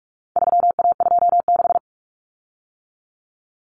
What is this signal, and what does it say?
Morse code, heard as a keyed tone, about 44 wpm, 700 Hz. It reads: 3A26
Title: Common Cut Numbers - 4-Digit 45wpm